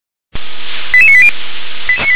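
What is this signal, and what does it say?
Five-tone selective call (ZVEI), heard as quick runs of five tones, tones of about 70 ms. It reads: 88980
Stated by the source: ZVEI five-tone call received via SDR.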